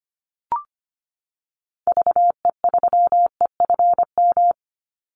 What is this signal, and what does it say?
Morse code, heard as a keyed tone, about 25 wpm, 700 Hz. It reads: VE3EFM
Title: Call Signs - 25wpm